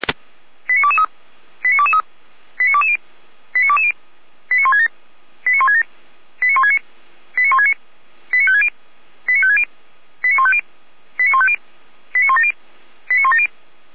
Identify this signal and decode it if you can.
Five-tone selective call (ZVEI), heard as quick runs of five tones, tones of about 70 ms. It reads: 89223 89223 89220 89220 89167 89167 89169 89169 89560 89560 89150 89150 89170 89170